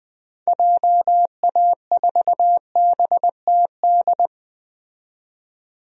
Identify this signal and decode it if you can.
Morse code, heard as a keyed tone, about 20 wpm, 700 Hz. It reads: JA4BTD
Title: DXCC-100 Call Signs - 6 Characters - 20wpm